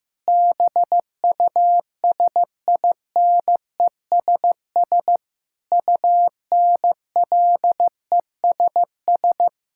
Morse code, heard as a keyed tone, about 15 wpm, 700 Hz. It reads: BUSINESS UNLESS
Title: Sentences from Top 1000 Words - Review Part 6 - 15wpm